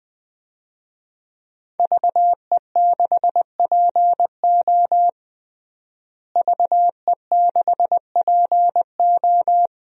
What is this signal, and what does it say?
Morse code, heard as a keyed tone, about 20 wpm, 700 Hz. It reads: VE6PO VE6PO
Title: DXCC-100 Call Signs - 5 Characters - Repeated 3x - 20wpm